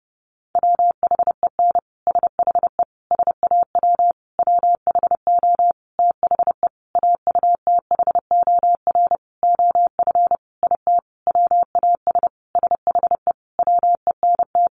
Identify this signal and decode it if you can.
Morse code, heard as a keyed tone, about 30 wpm, 700 Hz. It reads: WHEN SHE SAW WHO THE AUTHOR OF IT WAS SHE WENT
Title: Sentences from Top 1200 Words 30wpm